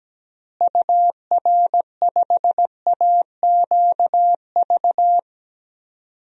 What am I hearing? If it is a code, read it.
Morse code, heard as a keyed tone, about 17 wpm, 700 Hz. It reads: UR5AQV
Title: DXCC-100 Call Signs - 6 Characters - Repeated 2x - 17wpm